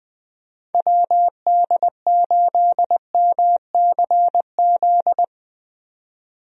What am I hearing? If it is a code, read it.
Morse code, heard as a keyed tone, about 20 wpm, 700 Hz. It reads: WD8MCZ
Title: USA Call Signs - 6 Characters - 20wpm